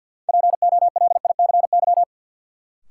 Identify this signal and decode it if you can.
Morse code, heard as a keyed tone, about 53 wpm, 700 Hz. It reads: POLICY
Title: Sentences from Top 1200 Words - New Word Drill Alt - 50wpm